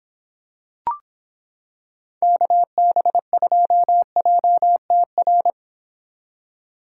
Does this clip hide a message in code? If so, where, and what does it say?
Yes: Morse code from 2.2 to 5.5 s, heard as a keyed tone, about 25 wpm, 700 Hz. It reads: KB2JTR